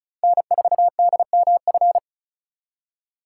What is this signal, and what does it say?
Morse code, heard as a keyed tone, about 36 wpm, 700 Hz. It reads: N4DMF
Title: USA Call Signs - 3x Character Spacing - 35wpm